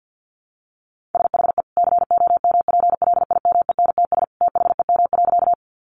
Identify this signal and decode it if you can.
Morse code, heard as a keyed tone, about 48 wpm, 700 Hz. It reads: SHE COMPLIMENTS THEM.
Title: Easy Phrases 50wpm